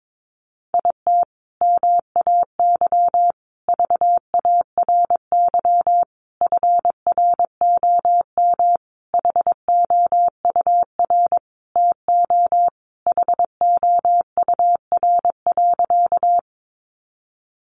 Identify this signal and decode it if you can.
Morse code, heard as a keyed tone, about 22 wpm, 700 Hz. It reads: IT MAY VARY FROM HOUR TO HOUR.
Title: Sentences from Top 1200 Words 22wpm